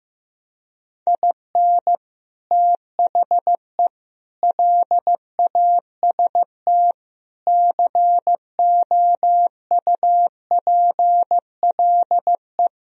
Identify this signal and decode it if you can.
Morse code, heard as a keyed tone, about 15 wpm, 700 Hz. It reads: IN THE LAST COUPLE
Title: Speed-Racing - Sets of 5 Words 15wpm